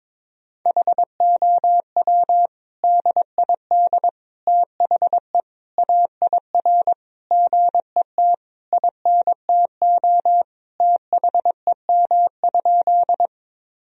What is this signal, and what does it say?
Morse code, heard as a keyed tone, about 22 wpm, 700 Hz. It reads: HOW DID THE AIR GET INTO THEM?
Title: Sentences from Top 1000 Words - Review Part 1 - 22wpm